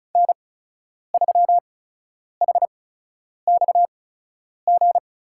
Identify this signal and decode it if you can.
Morse code, heard as a keyed tone, about 36 wpm, spaced as for 10 wpm, 700 Hz. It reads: N3HXG
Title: USA Call Signs - 8x Character Spacing - 35wpm